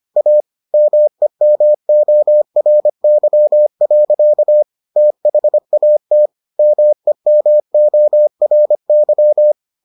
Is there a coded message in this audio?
Yes: Morse code from 0.2 to 9.5 s, heard as a keyed tone, about 25 wpm, 600 Hz. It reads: A MEMORY. THAT MEMORY